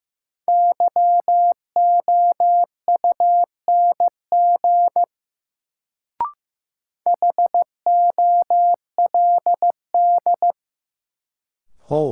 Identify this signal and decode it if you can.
Morse code, heard as a keyed tone, about 15 wpm, 700 Hz. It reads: YOUNG HOLD
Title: Sentences from Top 400 Words - New Word Drill - 15wpm